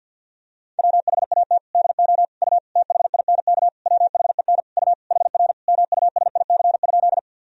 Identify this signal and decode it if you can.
Morse code, heard as a keyed tone, about 51 wpm, 700 Hz. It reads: WHAT DO U THINK WHEN U HR MUSIC?
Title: Sentences from Top 300 Words and Abbreviations 50wpm